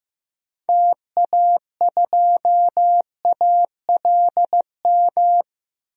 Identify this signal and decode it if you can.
Morse code, heard as a keyed tone, about 15 wpm, 700 Hz. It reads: TA2ALM
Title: DXCC-100 Call Signs - 6 Characters - 15wpm